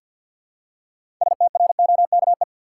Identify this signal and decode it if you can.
Morse code, heard as a keyed tone, about 47 wpm, 700 Hz. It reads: STROKE
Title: Six Letter Words - 50wpm